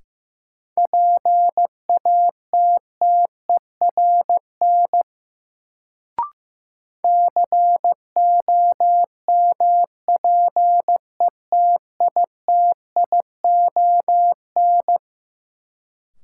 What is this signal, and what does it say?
Morse code, heard as a keyed tone, about 15 wpm, 700 Hz. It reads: PATTERN COMPETITION
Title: Sentences from Top 1000 Words - New Word Drill - 15wpm